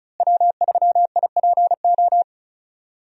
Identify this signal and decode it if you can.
Morse code, heard as a keyed tone, about 34 wpm, 700 Hz. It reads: W3IPO
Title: USA Call Signs - 5 Characters - Repeated 3x - 35wpm